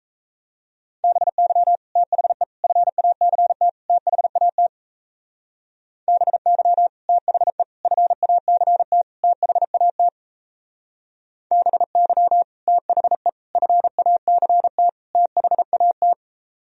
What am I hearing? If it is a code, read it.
Morse code, heard as a keyed tone, about 34 wpm, 700 Hz. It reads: BY THE FACT THAT BY THE FACT THAT BY THE FACT THAT
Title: Speed-Racing - Sets of 4 Words 28wpm